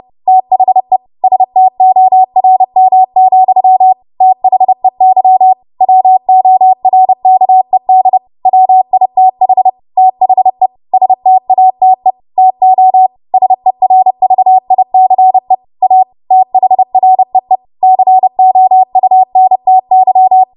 Morse code, heard as a keyed tone, about 30 wpm, 750 Hz. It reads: THE STORM, THEY WORKED WITH THE STATE TO SERVICE A THREE COUNTY